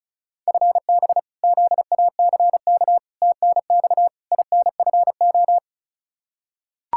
Morse code, heard as a keyed tone, about 34 wpm, 700 Hz. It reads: FB ZACK TNX INFO
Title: Practice QSO - Exchange 3 - More Info - 35wpm